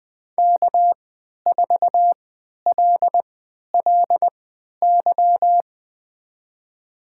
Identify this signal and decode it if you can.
Morse code, heard as a keyed tone, about 20 wpm, spaced as for 11 wpm, 700 Hz. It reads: K4LLY
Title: USA Call Signs - 5 Characters - 3x Character Spacing - 20wpm